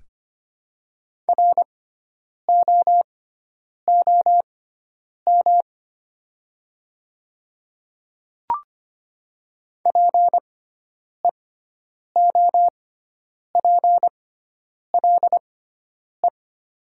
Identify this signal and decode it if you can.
Morse code, heard as a keyed tone, about 25 wpm, spaced as for 9 wpm, 690 Hz. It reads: ROOM PEOPLE
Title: 6x Character Spacing - Top 500 Words - 25wpm